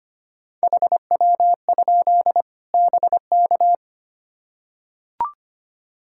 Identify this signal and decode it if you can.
Morse code, heard as a keyed tone, about 25 wpm, 700 Hz. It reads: HW? BK
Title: Practice QSO - Exchange 1 - Beginning - 25wpm